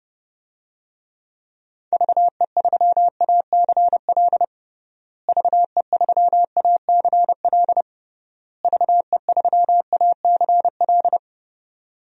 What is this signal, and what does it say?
Morse code, heard as a keyed tone, about 30 wpm, 700 Hz. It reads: VE3ACL VE3ACL VE3ACL
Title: DXCC-100 Call Signs - 6 Characters - Repeated 3x - 30wpm